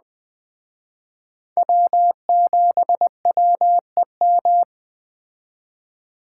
Morse code, heard as a keyed tone, about 20 wpm, 700 Hz. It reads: W7WEM
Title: Speed Racing - USA 1x3 Call Signs 17wpm